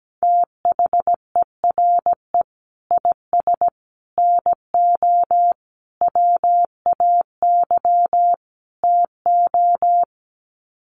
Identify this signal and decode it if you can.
Morse code, heard as a keyed tone, about 17 wpm, 700 Hz. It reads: THERE IS NO WAY TO
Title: Sets of 5 Words Encore 17wpm